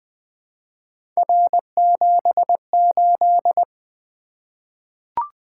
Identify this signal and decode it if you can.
Morse code, heard as a keyed tone, about 20 wpm, 700 Hz. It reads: R78
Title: Three Letter-Number Combinations - 20wpm